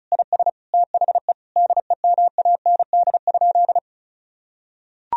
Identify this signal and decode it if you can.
Morse code, heard as a keyed tone, about 34 wpm, 700 Hz. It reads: IS THE DEMAND?
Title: Sentences from Top 1000 Words - Review Part 4 - 35wpm